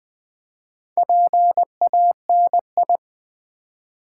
Morse code, heard as a keyed tone, about 20 wpm, 700 Hz. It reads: PANI
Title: Build A Word - 1 Letter At A Time - 3 to 5-Letter Words - 20wpm